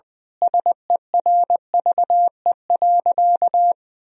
Morse code, heard as a keyed tone, about 20 wpm, 700 Hz. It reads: SERVE.
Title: Sentences from Top 1000 Words - Review Part 4 - 20wpm